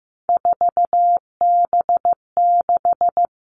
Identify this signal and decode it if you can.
Morse code, heard as a keyed tone, about 15 wpm, 700 Hz. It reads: 4B6
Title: All Cut Numbers - 3-Digit 15wpm